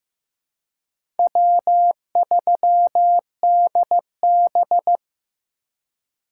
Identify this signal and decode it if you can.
Morse code, heard as a keyed tone, about 15 wpm, 700 Hz. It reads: W3DB
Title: USA Call Signs - 4x Character Spacing - 15wpm